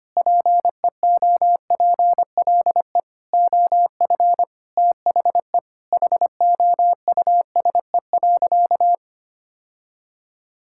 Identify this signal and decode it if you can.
Morse code, heard as a keyed tone, about 25 wpm, 700 Hz. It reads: PEOPLE OF THE HOUSE.
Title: Sentences from Top 1000 Words - Review Part 5 - 25wpm